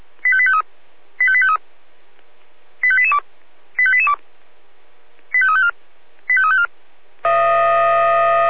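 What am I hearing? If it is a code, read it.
Five-tone selective call (ZVEI), heard as quick runs of five tones, tones of about 70 ms. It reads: 85853 85853 85882 85882 85354 85354
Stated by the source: ZVEI five-tone selective call received on SDR.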